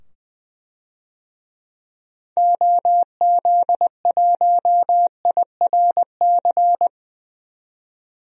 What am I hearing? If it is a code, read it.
Morse code, heard as a keyed tone, about 20 wpm, 700 Hz. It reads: OZ1IRC
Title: DXCC-100 Call Signs - 6 Characters - 20wpm